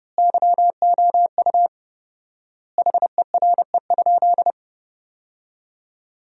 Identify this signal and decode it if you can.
Morse code, heard as a keyed tone, about 30 wpm, 700 Hz. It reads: YOU HERE?
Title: 4x Word-Spacing - Sentences from Top 500 Words - 30wpm